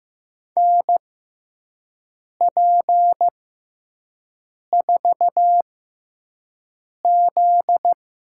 Morse code, heard as a keyed tone, about 15 wpm, spaced as for 5 wpm, 700 Hz. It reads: NP4Z
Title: USA Call Signs - 4 Characters - 6x Character Spacing - 15wpm